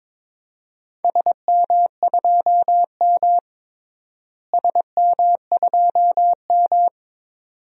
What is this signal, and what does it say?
Morse code, heard as a keyed tone, about 22 wpm, 700 Hz. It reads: SM2M SM2M
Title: DXCC-100 Call Signs - 4 Characters - Repeated 2x - 22wpm